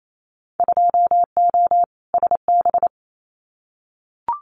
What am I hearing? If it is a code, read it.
Morse code, heard as a keyed tone, about 28 wpm, 700 Hz. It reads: 2O SB
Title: Winter Field Day Exchanges 28wpm